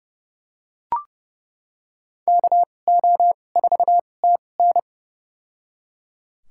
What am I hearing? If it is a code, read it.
Morse code, heard as a keyed tone, about 30 wpm, spaced as for 22 wpm, 700 Hz. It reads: KO4TN
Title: USA Call Signs - 5 Characters - 2x Character Spacing - 30wpm